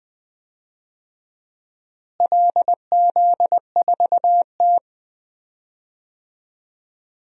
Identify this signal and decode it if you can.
Morse code, heard as a keyed tone, about 20 wpm, 700 Hz. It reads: LZ4T
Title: DXCC-100 Call Signs - 4 Characters - Repeated 3x - 20wpm